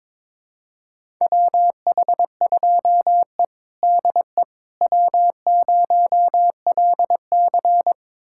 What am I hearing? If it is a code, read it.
Morse code, heard as a keyed tone, about 22 wpm, 700 Hz. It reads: WH2E DE W0LC
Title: USA Callsign DE Callsign - 4 Characters - 22wpm